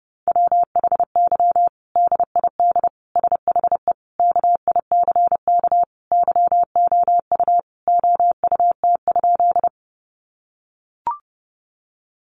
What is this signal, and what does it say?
Morse code, heard as a keyed tone, about 30 wpm, 700 Hz. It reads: WHY DID SHE KICK YOU OUT?
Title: Sentences from Top 1300 Words 30wpm